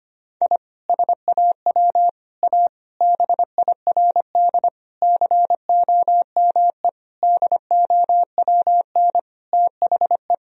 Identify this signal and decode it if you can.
Morse code, heard as a keyed tone, about 25 wpm, 700 Hz. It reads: I SAW A BIRD COME DOWN THE
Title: Sentences from Top 1000 Words - Review Part 2 - 25wpm